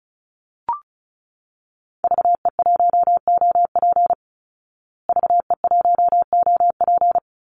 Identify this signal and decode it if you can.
Morse code, heard as a keyed tone, about 34 wpm, 700 Hz. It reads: VE1OP VE1OP
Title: Call Signs Repeated 3x - 35wpm